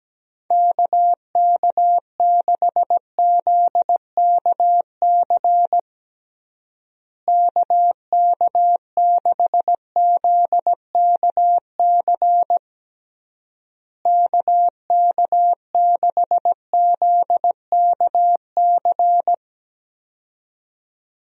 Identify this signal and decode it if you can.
Morse code, heard as a keyed tone, about 17 wpm, 700 Hz. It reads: KK6ZKC KK6ZKC KK6ZKC
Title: USA Call Signs - 6 Characters - Repeated 3x - 17wpm